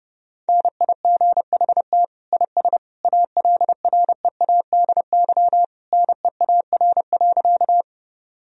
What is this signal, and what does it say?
Morse code, heard as a keyed tone, about 32 wpm, 700 Hz. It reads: NIGHT IS ALREADY NEAR.
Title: Sentences from Top 1000 Words - Review Part 1 - 30wpm